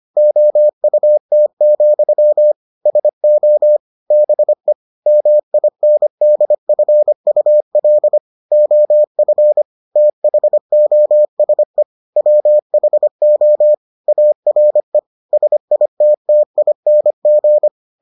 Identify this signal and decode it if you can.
Morse code, heard as a keyed tone, about 25 wpm, 600 Hz. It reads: OUT, SO BE MINDFUL OF THOSE WHO ARE SITTING